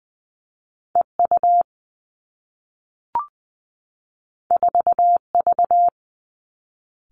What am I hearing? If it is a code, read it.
Morse code, heard as a keyed tone, about 20 wpm, 700 Hz. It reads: E U 4 V
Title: Two Challenging Letter-Number Combinations 20wpm